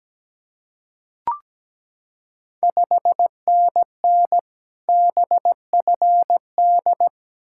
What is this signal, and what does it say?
Morse code, heard as a keyed tone, about 17 wpm, 700 Hz. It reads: 5NN BFD
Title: Ontario QSO Party Exchange 17wpm